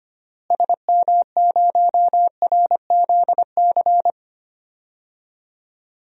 Morse code, heard as a keyed tone, about 25 wpm, 700 Hz. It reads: SM0RZC
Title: DXCC-100 Call Signs - 6 Characters - Repeated 2x - 25wpm